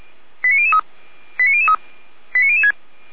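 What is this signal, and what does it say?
Five-tone selective call (ZVEI), heard as quick runs of five tones, tones of about 70 ms. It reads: 89003 89003 89006
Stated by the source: ZVEI five-tone selective call 89014 received with an SDR.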